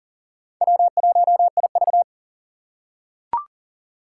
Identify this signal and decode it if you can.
Morse code, heard as a keyed tone, about 37 wpm, 700 Hz. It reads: W1IV
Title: Four Challenging Letter-Number Combinations 40wpm